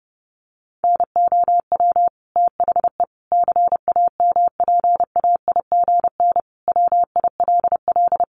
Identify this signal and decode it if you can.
Morse code, heard as a keyed tone, about 30 wpm, 700 Hz. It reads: NOW THE CAMPAIGN WILL
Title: Sentences from Top 1200 Words 30wpm